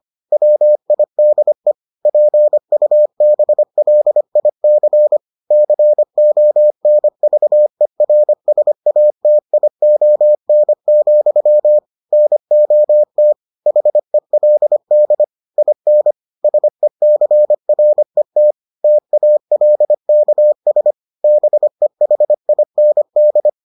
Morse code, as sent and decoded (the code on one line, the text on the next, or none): .-- .. -.. / .--. ..- -... .-.. .. -.-. / -.-. --- -. ...- . .-. ... .- - .. --- -. --..-- / -. --- - / .... .-.. -.. / .. -. / ... -.-. .-. - / - .- .-.. -.- ... / -... .... .. -. -..
WIDE PUBLIC CONVERSATION, NOT HELD IN SECRET TALKS BEHIND